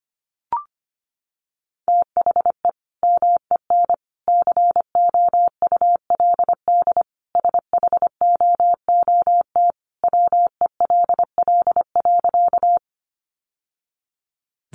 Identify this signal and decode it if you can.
Morse code, heard as a keyed tone, about 25 wpm, 700 Hz. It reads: THE MEN COULD SHOOT WELL.
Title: Sentences from Top 1200 Words 25wpm